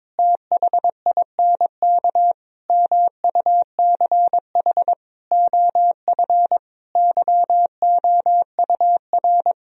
Morse code, heard as a keyed tone, about 22 wpm, 700 Hz. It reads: THINK MUCH OF YOUR
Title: Sentences from Top 1000 Words - Review Part 2 - 22wpm